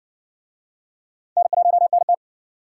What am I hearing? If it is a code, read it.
Morse code, heard as a keyed tone, about 56 wpm, 700 Hz. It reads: N1NT